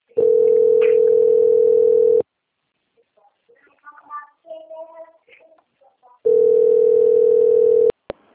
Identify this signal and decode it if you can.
Ringback tone (North America), 440 + 480 Hz.